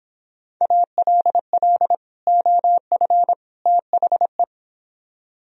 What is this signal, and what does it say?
Morse code, heard as a keyed tone, about 23 wpm, 700 Hz. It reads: ALL OF THE